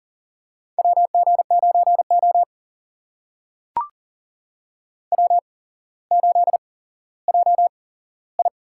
Morse code, heard as a keyed tone, about 38 wpm, spaced as for 17 wpm, 700 Hz. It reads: WG9O W8JI
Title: USA Call Signs - 4 Characters - 8x Character Spacing - 40wpm